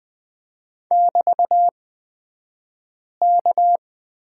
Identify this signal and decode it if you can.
Morse code, heard as a keyed tone, about 20 wpm, spaced as for 5 wpm, 700 Hz. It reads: <BT>K
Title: 223 Lesson 36d - BT - Two Characters 20wpm